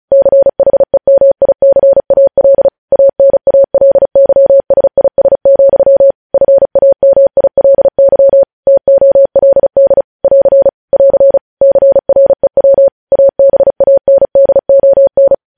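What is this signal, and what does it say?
Morse code, heard as a keyed tone, about 36 wpm, 550 Hz. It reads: CHEMICAL ANALYSIS, FAMILY TOLD <AR> <AR> CREW ABANDON
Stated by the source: Morse code news headlines sent at 35 words per minute.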